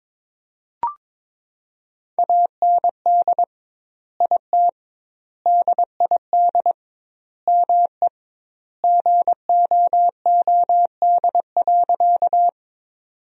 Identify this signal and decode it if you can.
Morse code, heard as a keyed tone, about 22 wpm, 700 Hz. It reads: AND IT DID ME GOOD.
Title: 2x Word-Spacing - Sentences from Top 200 Words - 22wpm